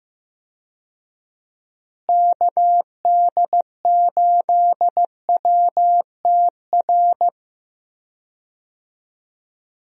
Morse code, heard as a keyed tone, about 15 wpm, 700 Hz. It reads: KD8WTR
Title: USA Call Signs - 6 Characters - Repeated 3x - 15wpm